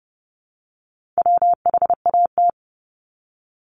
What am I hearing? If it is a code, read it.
Morse code, heard as a keyed tone, about 30 wpm, 700 Hz. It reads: WHAT
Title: Multi-letter Phonemes 30wpm